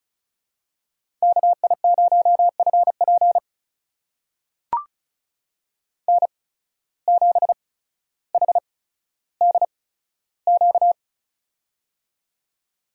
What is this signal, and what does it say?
Morse code, heard as a keyed tone, about 36 wpm, spaced as for 15 wpm, 700 Hz. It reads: KI0FP N7HDQ
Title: USA Call Signs - 8x Character Spacing - 35wpm